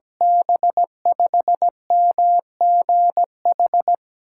Morse code, heard as a keyed tone, about 17 wpm, 700 Hz. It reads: B5MGH
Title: USA Call Signs - 8x Character Spacing - 17wpm